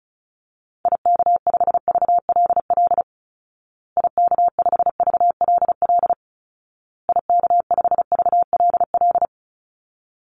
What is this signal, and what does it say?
Morse code, heard as a keyed tone, about 34 wpm, 700 Hz. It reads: IK5VLL IK5VLL IK5VLL
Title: Call Signs Repeated 2x - 35wpm